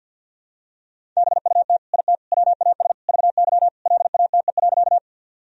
Morse code, heard as a keyed tone, about 49 wpm, 700 Hz. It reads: BUT IT WAS VY LATE.
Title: Sentences from Top 300 Words and Abbreviations 50wpm